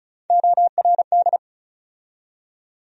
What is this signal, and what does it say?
Morse code, heard as a keyed tone, about 36 wpm, 700 Hz. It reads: ORD
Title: USA Call Sign Postfixes - 3 Letters 35wpm